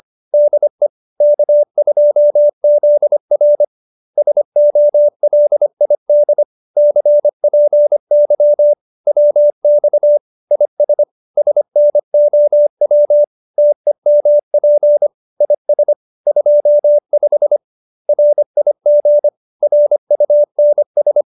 Morse code, heard as a keyed tone, about 25 wpm, 590 Hz. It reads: DE K2ZR SOLID CPY WX IS SNOW TEMP IS 25 RIG RUNS
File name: QSO302 25